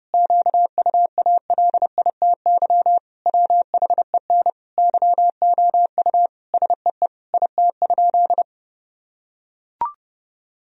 Morse code, heard as a keyed tone, about 32 wpm, 700 Hz. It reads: QUALITY WHEN YOU SEE IT?
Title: Sentences from Top 1000 Words - Review Part 4 - 30wpm